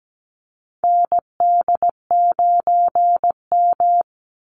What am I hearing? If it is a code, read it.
Morse code, heard as a keyed tone, about 17 wpm, 700 Hz. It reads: ND9M
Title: Call Signs Repeated 2x - 17wpm